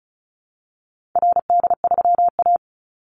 Morse code, heard as a keyed tone, about 34 wpm, 700 Hz. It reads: RD3A
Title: Call Signs Repeated 2x - 35wpm